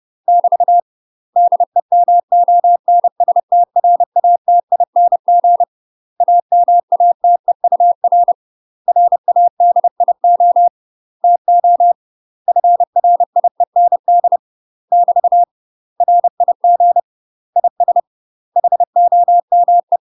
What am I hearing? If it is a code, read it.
Morse code, heard as a keyed tone, about 30 wpm, 700 Hz. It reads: <BT> DEMONSTRATING AMATEUR RADIO TO FRIEND <BT> RIG IS HOME
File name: UK MASTER QSOs TRAINING / 30WPM / UK MASTER QSOs 0028